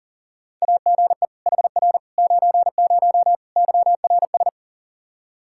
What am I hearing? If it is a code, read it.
Morse code, heard as a keyed tone, about 40 wpm, 700 Hz. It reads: AGE HR 90 YRS
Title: Practice QSO - Exchange 3 - More Info - 40wpm